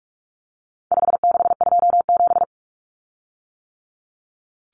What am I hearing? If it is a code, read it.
Morse code, heard as a keyed tone, about 44 wpm, 700 Hz. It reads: 5627
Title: Common Cut Numbers - 4-Digit 45wpm